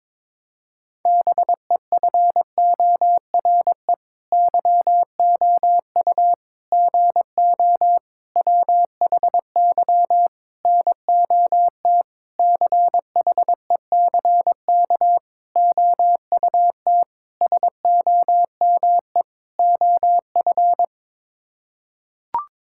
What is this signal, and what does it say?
Morse code, heard as a keyed tone, about 22 wpm, 700 Hz. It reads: BEFORE YOU GO WHY NOT CHECK OUT SOME OF
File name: Sets of 9 Words - 22wpm